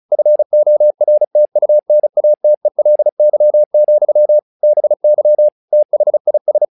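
Morse code, heard as a keyed tone, about 34 wpm, 600 Hz. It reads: FORTUNATELY, BY THIS
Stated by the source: Morse practice text "Tower Trouble" sent at 35 WPM, from the SKCC - Straight Key Century Club learning center audio files.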